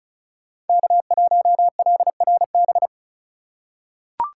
Morse code, heard as a keyed tone, about 34 wpm, 700 Hz. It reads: K1LRB
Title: USA Call Signs - 5 Characters - 6x Character Spacing - 35wpm